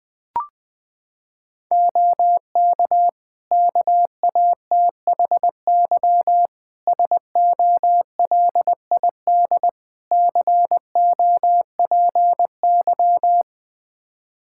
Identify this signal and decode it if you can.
Morse code, heard as a keyed tone, about 21 wpm, 700 Hz. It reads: OK KATHY SOLID COPY
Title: Practice QSO - Exchange 3 - More Info - 20wpm